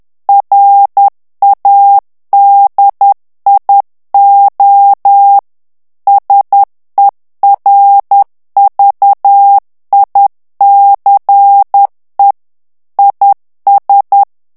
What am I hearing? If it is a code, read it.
Morse code, heard as a keyed tone, about 10 wpm, 800 Hz. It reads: RADIO SERVICE IS
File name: ANGLAIS texte clair 3 vitesse 10 mots minute